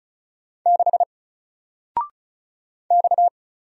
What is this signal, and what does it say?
Morse code, heard as a keyed tone, about 36 wpm, 700 Hz. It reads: 6 X
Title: Single Character (Letter:Number:Punctuation:Prosign) Alt 35wpm